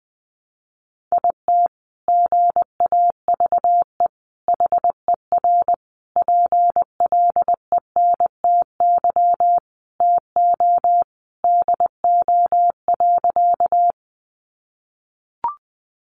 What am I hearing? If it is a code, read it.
Morse code, heard as a keyed tone, about 19 wpm, 700 Hz. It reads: IT GAVE HER PLENTY TO DO.
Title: Sentences from Top 1300 Words 20wpm